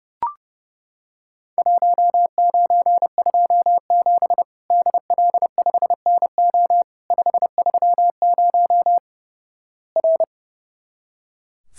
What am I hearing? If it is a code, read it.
Morse code, heard as a keyed tone, about 32 wpm, 700 Hz. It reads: 1927 DL5NO 530 R
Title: Worked All Europe DX Contest - QTC Exchange - 30wpm